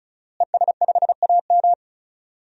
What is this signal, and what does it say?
Morse code, heard as a keyed tone, about 34 wpm, 700 Hz. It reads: ES5AM
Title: DXCC-100 Call Signs - 5 Characters - Repeated 3x - 35wpm